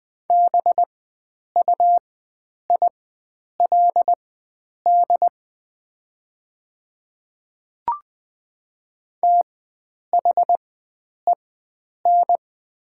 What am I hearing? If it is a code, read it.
Morse code, heard as a keyed tone, about 20 wpm, spaced as for 9 wpm, 700 Hz. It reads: BUILD THEN
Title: 4x Character Spacing - Top 500 Words - 20wpm